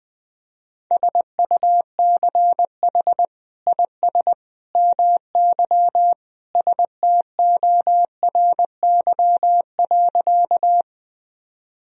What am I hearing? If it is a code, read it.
Morse code, heard as a keyed tone, about 21 wpm, 700 Hz. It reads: SUCH IS MY STORY.
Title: Sentences from Top 1000 Words - Review Part 1 - 20wpm